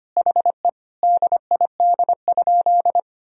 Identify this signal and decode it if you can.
Morse code, heard as a keyed tone, about 25 wpm, 700 Hz. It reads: HE DID?
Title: Sentences from Top 100 Words and Abbreviations 25wpm